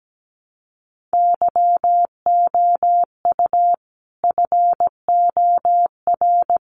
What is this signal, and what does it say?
Morse code, heard as a keyed tone, about 17 wpm, 700 Hz. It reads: YOU FOR
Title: Sets of 5 Words 17wpm